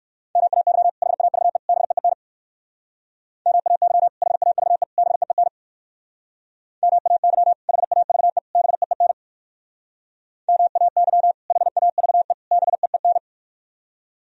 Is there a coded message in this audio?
Yes: Morse code from 0.4 to 13.2 s, heard as a keyed tone, about 58 wpm, 700 Hz. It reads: MAY HAVE BEEN MAY HAVE BEEN MAY HAVE BEEN MAY HAVE BEEN